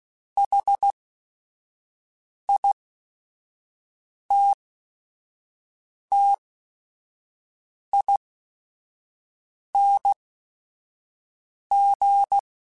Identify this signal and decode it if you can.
Morse code, heard as a keyed tone, about 16 wpm, spaced as for 5 wpm, 800 Hz. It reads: HITTING